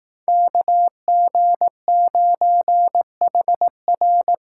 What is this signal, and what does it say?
Morse code, heard as a keyed tone, about 18 wpm, 700 Hz. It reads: KG9HR
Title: Speed Racing - USA 2x2 Call Signs 15wpm